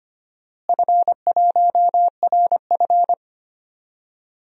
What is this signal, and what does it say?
Morse code, heard as a keyed tone, about 25 wpm, 700 Hz. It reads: F1RF
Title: DXCC-100 Call Signs - 4 Characters - Repeated 2x - 25wpm